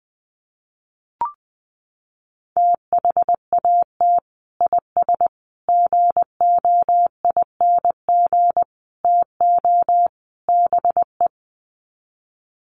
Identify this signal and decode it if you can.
Morse code, heard as a keyed tone, about 20 wpm, 700 Hz. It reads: THAT IS GOING TO BE
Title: Sets of 5 Words Encore2 20wpm